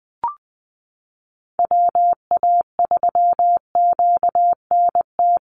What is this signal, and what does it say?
Morse code, heard as a keyed tone, about 20 wpm, 700 Hz. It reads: WA3QNT
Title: Call Signs Repeated 2x - 20wpm